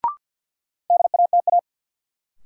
Morse code, heard as a keyed tone, about 52 wpm, 700 Hz. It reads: DATA
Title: Sentences from Top 1000 Words - Review Part 3 - New Word Drill - 50wpm